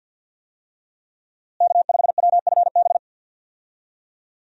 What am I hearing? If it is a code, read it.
Morse code, heard as a keyed tone, about 52 wpm, 700 Hz. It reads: K5WFB
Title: USA Call Signs - 5 Characters - Repeated 2x - 50wpm